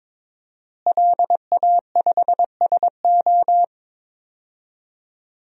Morse code, heard as a keyed tone, about 22 wpm, 700 Hz. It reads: LA5SO
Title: DXCC-100 Call Signs - 5 Characters - Repeated 2x - 22wpm